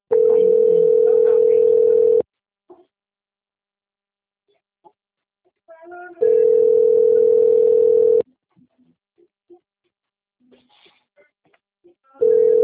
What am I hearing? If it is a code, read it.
Ringback tone (North America), 440 + 480 Hz.
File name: out- - - - - .41108